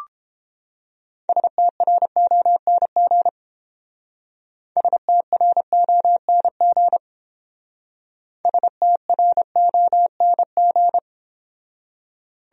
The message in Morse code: STRONG STRONG STRONG

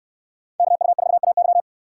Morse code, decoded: KU4AY

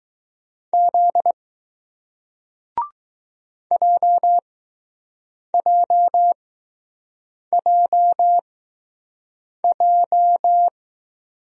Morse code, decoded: Z JJJJ

